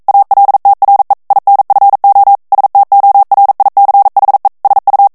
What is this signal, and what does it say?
Morse code, heard as a keyed tone, about 42 wpm, 800 Hz. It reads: ALTRE INFO STORICHE SU
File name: ITALIEN texte clair 1 vitesse 40 mots minute